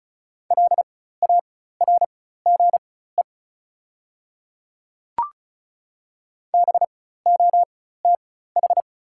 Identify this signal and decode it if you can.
Morse code, heard as a keyed tone, about 36 wpm, spaced as for 16 wpm, 690 Hz. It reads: LARGE BOTH
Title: 4x Character Spacing - Top 500 Words - 35wpm